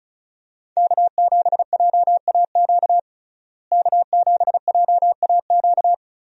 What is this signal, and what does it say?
Morse code, heard as a keyed tone, about 36 wpm, 700 Hz. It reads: K7JAQ K7JAQ
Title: USA Call Signs - 5 Characters - Repeated 2x - 35wpm